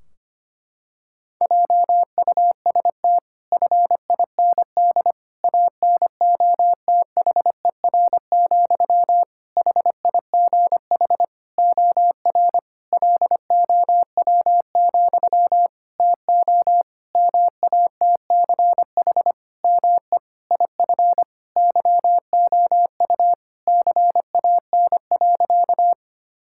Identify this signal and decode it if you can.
Morse code, heard as a keyed tone, about 25 wpm, 700 Hz. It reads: JUST FIND ANOTHER, HIGH OR LOW, TO MATCH ME IF YOU CAN.
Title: Sentences from Top 1600 Words 25wpm